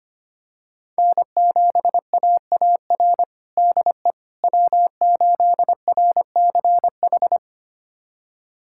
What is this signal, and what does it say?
Morse code, heard as a keyed tone, about 25 wpm, 700 Hz. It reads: N7AAR DE W8RCH
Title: USA Callsign DE Callsign - 5 Characters - 25wpm